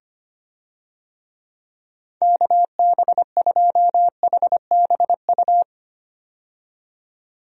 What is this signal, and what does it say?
Morse code, heard as a keyed tone, about 25 wpm, 700 Hz. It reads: KB2HBU
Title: USA Call Signs - 6 Characters - 25wpm